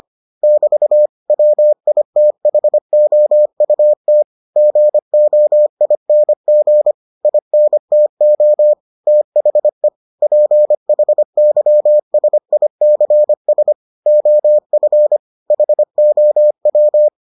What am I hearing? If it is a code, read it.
Morse code, heard as a keyed tone, about 25 wpm, 600 Hz. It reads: <BT> WITHOUT GOING INTO THE PHYSICS OF HOW